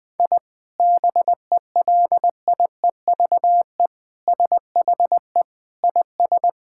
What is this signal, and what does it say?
Morse code, heard as a keyed tone, about 21 wpm, 700 Hz. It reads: I BELIEVE SHE IS
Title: Sentences from Top 1000 Words - Review Part 5 - 20wpm